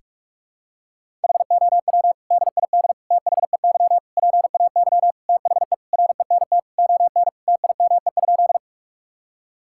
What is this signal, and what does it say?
Morse code, heard as a keyed tone, about 46 wpm, 700 Hz. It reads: HOW DID THEY PAY THE RENT ON TIME?
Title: Sentences from Top 1000 Words - Review Part 5 - 45wpm